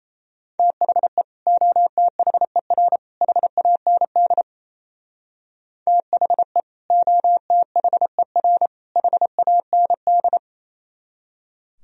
Morse code, heard as a keyed tone, about 34 wpm, 700 Hz. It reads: THE OTHER HAND THE OTHER HAND
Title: Speed-Racing - Sets of 3 Words 28wpm